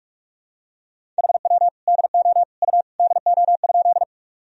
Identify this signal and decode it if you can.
Morse code, heard as a keyed tone, about 46 wpm, 700 Hz. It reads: HW DO U DO?
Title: Sentences from Top 100 Words and Abbreviations 45wpm